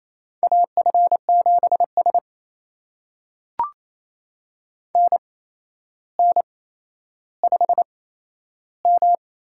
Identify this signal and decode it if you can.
Morse code, heard as a keyed tone, about 28 wpm, spaced as for 12 wpm, 700 Hz. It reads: AF7S NN5M
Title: USA Call Signs - 4 Characters - 8x Character Spacing - 28wpm